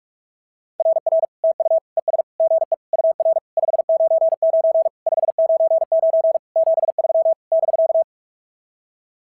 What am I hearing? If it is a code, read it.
Morse code, heard as a keyed tone, about 46 wpm, 690 Hz. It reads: RR TU ES GE UR 599 599 73 <BK>
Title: SOTA QSO Practice 45wpm